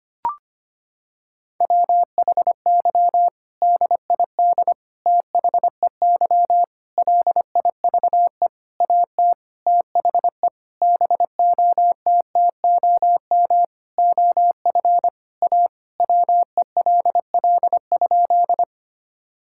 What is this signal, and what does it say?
Morse code, heard as a keyed tone, about 25 wpm, 700 Hz. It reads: WHY DID THEY LIVE AT THE BOTTOM OF A WELL?
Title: Sentences from Top 1000 Words - Review Part 6 - 25wpm